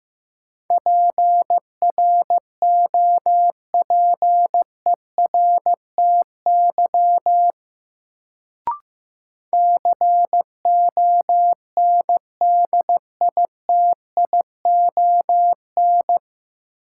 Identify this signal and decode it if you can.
Morse code, heard as a keyed tone, about 15 wpm, 700 Hz. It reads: PROPERTY CONDITION
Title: Sentences from Top 1000 Words - New Word Drill - 15wpm